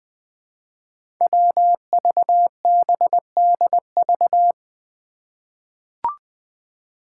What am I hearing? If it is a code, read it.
Morse code, heard as a keyed tone, about 20 wpm, 700 Hz. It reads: WVBDV
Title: Five Challenging Letter-Number Combinations 20wpm